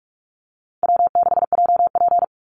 Morse code, heard as a keyed tone, about 44 wpm, 700 Hz. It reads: W6JP
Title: NA SOTA Chaser Callsigns 45wpm